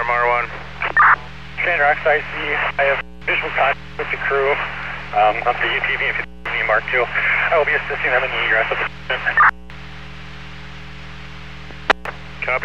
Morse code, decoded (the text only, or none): none